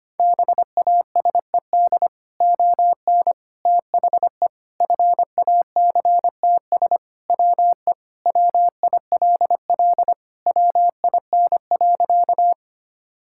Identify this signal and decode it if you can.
Morse code, heard as a keyed tone, about 25 wpm, 700 Hz. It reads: BASED ON THE FACTS WE WILL WIN.
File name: practice / 25wpm / Sentences with 3-Word Phrases - 25wpm